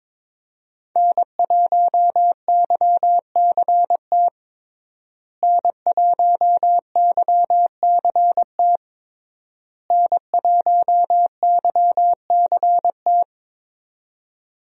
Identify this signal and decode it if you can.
Morse code, heard as a keyed tone, about 22 wpm, 700 Hz. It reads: N1YCT N1YCT N1YCT
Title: USA Call Signs - 5 Characters - Repeated 3x - 22wpm